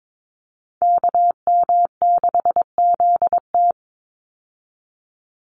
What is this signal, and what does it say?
Morse code, heard as a keyed tone, about 22 wpm, 700 Hz. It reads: KM6ZT
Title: Call Signs Repeated 2x - 22wpm